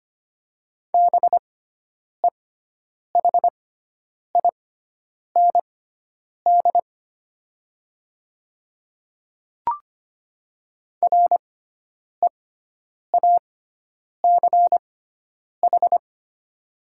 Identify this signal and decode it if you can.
Morse code, heard as a keyed tone, about 25 wpm, spaced as for 9 wpm, 700 Hz. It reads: BEHIND REACH
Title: 6x Character Spacing - Top 500 Words - 25wpm